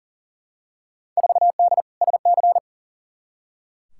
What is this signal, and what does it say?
Morse code, heard as a keyed tone, about 40 wpm, 700 Hz. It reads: 4D SC
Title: Field Day Exchanges Encore - 40wpm